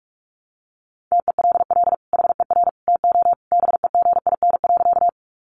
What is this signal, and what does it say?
Morse code, heard as a keyed tone, about 44 wpm, 700 Hz. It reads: TELL HER TO BEGIN.
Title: Sentences from Top 300 Words 45wpm